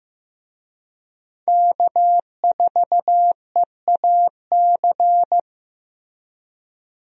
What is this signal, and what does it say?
Morse code, heard as a keyed tone, about 15 wpm, 700 Hz. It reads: K4EAC
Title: USA Call Signs - 5 Characters - Repeated 3x - 15wpm